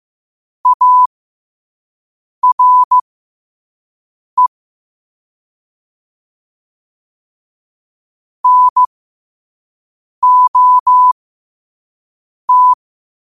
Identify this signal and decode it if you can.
Morse code, heard as a keyed tone, about 15 wpm, spaced as for 5 wpm, 1000 Hz. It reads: ARE NOT
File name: 5wpm